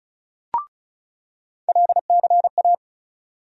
Morse code, heard as a keyed tone, about 36 wpm, 700 Hz. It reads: LCA
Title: USA Call Sign Postfixes - 3 Letters 35wpm